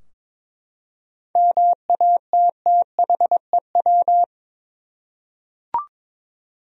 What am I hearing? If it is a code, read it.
Morse code, heard as a keyed tone, about 21 wpm, 700 Hz. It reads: MATTHEW
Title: USA Common Names - 22wpm